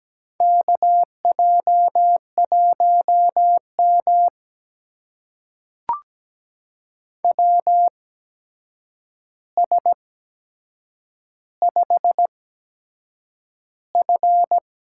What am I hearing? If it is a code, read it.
Morse code, heard as a keyed tone, about 17 wpm, spaced as for 7 wpm, 700 Hz. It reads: KJ1M WS5F
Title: USA Call Signs - 4 Characters - 8x Character Spacing - 17wpm